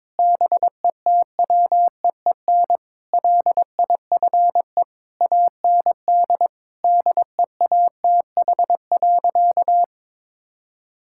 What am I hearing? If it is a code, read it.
Morse code, heard as a keyed tone, about 22 wpm, 700 Hz. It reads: BETWEEN LIFE AND DEATH.
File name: practice / 22wpm / Sentences with 8-Word Phrases - 22wpm